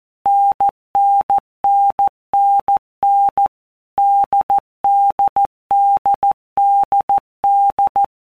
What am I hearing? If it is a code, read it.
Morse code, heard as a keyed tone, about 14 wpm, 800 Hz. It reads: NNNNN DDDDD
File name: Basic Letter Groups By Blocks / Groups@13 / AWUNDG@13